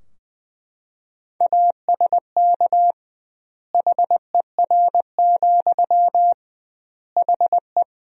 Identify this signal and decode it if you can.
Morse code, heard as a keyed tone, about 21 wpm, 700 Hz. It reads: ASK HER, HE
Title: 2x Word-Spacing - Sentences from Top 200 Words - 20wpm